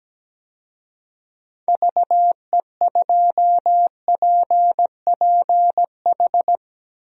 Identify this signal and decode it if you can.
Morse code, heard as a keyed tone, about 17 wpm, 700 Hz. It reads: VE2PPH
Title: DXCC-100 Call Signs - 6 Characters - Repeated 3x - 17wpm